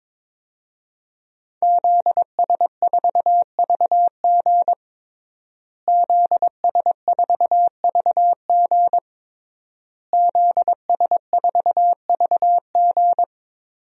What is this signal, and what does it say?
Morse code, heard as a keyed tone, about 22 wpm, 700 Hz. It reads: ZS4VG ZS4VG ZS4VG
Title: DXCC-100 Call Signs - 5 Characters - Repeated 3x - 22wpm